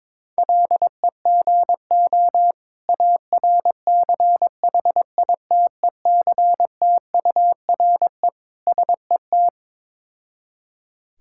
Morse code, heard as a keyed tone, about 22 wpm, 690 Hz. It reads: LEGO ARCHITECTURE SET
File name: Difficult Words with Hints - 22wpm